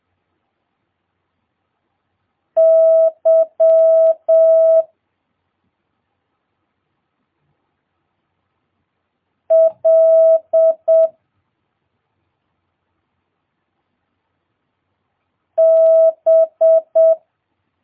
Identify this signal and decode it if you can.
Morse code, heard as a keyed tone, about 7 wpm, 650 Hz. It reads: Y L B
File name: jh8cbh / 2022cw